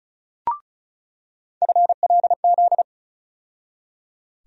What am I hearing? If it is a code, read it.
Morse code, heard as a keyed tone, about 36 wpm, 700 Hz. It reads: FLZ
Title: USA Call Sign Postfixes - 3 Letters 35wpm